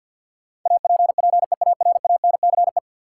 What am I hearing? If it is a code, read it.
Morse code, heard as a keyed tone, about 49 wpm, 700 Hz. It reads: APPEARANCE
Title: Sentences from Top 1100 Words - New Word Drill - 50wpm